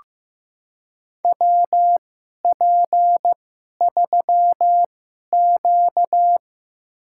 Morse code, heard as a keyed tone, about 15 wpm, spaced as for 11 wpm, 700 Hz. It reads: WP3Q
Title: USA Call Signs - 4 Characters - 2x Character Spacing - 15wpm